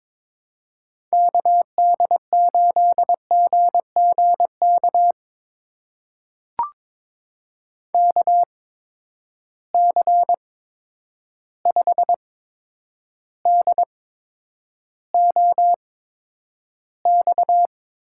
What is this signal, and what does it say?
Morse code, heard as a keyed tone, about 22 wpm, spaced as for 9 wpm, 700 Hz. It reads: KD8GGK KC5DOX
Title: USA Call Signs - 6 Characters - 8x Character Spacing - 22wpm